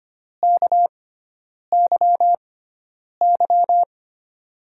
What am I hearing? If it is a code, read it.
Morse code, heard as a keyed tone, about 25 wpm, spaced as for 9 wpm, 690 Hz. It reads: KYY
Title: USA Call Signs - 5 Characters - 6x Character Spacing - 25wpm